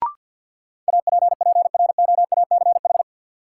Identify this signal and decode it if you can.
Morse code, heard as a keyed tone, about 53 wpm, 700 Hz. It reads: APPROACH
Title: Sentences from Top 1000 Words - Review Part 6 - New Word Drill - 50wpm